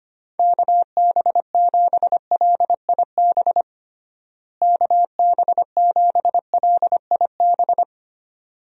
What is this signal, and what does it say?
Morse code, heard as a keyed tone, about 25 wpm, 700 Hz. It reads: KB7LIB KB7LIB
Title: USA Call Signs - 6 Characters - Repeated 2x - 25wpm